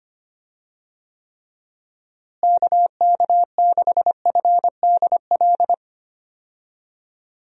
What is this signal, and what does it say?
Morse code, heard as a keyed tone, about 25 wpm, 700 Hz. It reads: KK6FDL
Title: USA Call Signs - 6 Characters - Repeated 3x - 25wpm